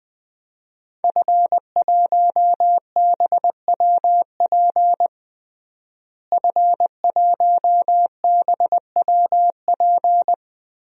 Morse code, heard as a keyed tone, about 20 wpm, 700 Hz. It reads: F1BWP F1BWP
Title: DXCC-100 Call Signs - 5 Characters - Repeated 2x - 20wpm